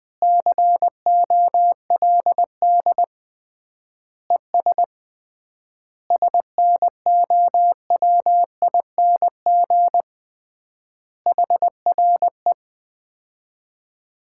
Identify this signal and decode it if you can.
Morse code, heard as a keyed tone, about 20 wpm, 700 Hz. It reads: COLD ES SNOWING HRE
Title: Common QSO Phrases - 3x Word-Spacing - 20wpm